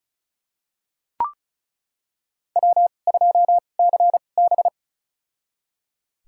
Morse code, heard as a keyed tone, about 36 wpm, spaced as for 25 wpm, 700 Hz. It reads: W2CB
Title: USA Call Signs - 4 Characters - 2x Character Spacing - 35wpm